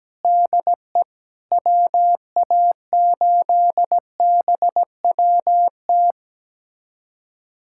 Morse code, heard as a keyed tone, about 17 wpm, 700 Hz. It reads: DE WA8BWT
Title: USA Callsign DE Callsign 17wpm